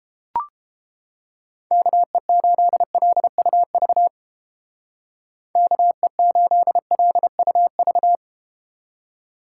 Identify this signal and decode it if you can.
Morse code, heard as a keyed tone, about 31 wpm, 700 Hz. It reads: KE8LUV KE8LUV